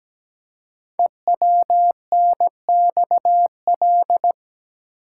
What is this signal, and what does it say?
Morse code, heard as a keyed tone, about 17 wpm, 700 Hz. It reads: EWNXL
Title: Code Groups - 5 Letters - 17wpm